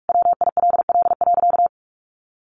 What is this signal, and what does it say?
Morse code, heard as a keyed tone, about 44 wpm, 700 Hz. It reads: WILL.
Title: Sentences from Top 1400 Words 45wpm